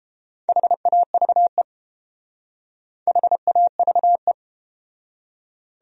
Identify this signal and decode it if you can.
Morse code, heard as a keyed tone, about 30 wpm, 700 Hz. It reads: HAVE HAVE